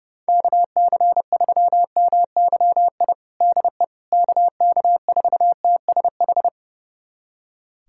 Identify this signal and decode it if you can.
Morse code, heard as a keyed tone, about 30 wpm, 700 Hz. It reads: KC3MYI DE KK4TSH
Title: USA Callsign DE Callsign - 6 Characters - 30wpm